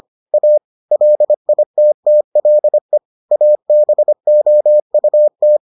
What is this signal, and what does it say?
Morse code, heard as a keyed tone, about 25 wpm, 600 Hz. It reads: A LITTLE ABOUT